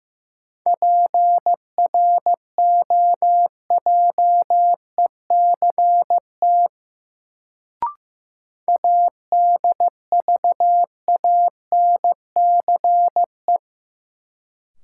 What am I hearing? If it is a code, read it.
Morse code, heard as a keyed tone, about 16 wpm, 700 Hz. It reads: PROJECT ADVANCE
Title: Sentences from Top 1400 Words - New Word Drill Alt - 15wpm